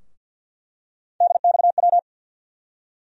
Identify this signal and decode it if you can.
Morse code, heard as a keyed tone, about 47 wpm, 700 Hz. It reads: DXW
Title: USA Call Sign Postfixes - 3 Letters 50wpm